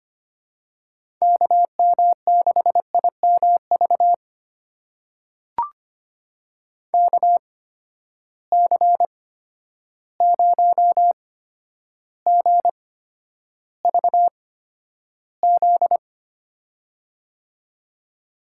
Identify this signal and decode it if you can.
Morse code, heard as a keyed tone, about 25 wpm, spaced as for 11 wpm, 700 Hz. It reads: KM6IMV KC0GVZ
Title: USA Call Signs - 6 Characters - 8x Character Spacing - 25wpm